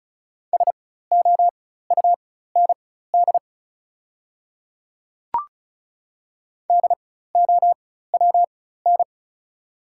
Morse code, heard as a keyed tone, about 36 wpm, spaced as for 16 wpm, 700 Hz. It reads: SOUND DOWN
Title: 4x Character Spacing - Top 100 Words - 35wpm